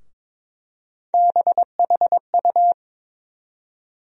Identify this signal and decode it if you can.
Morse code, heard as a keyed tone, about 22 wpm, 700 Hz. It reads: BHU
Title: USA Call Sign Postfixes - 3 Letters 22wpm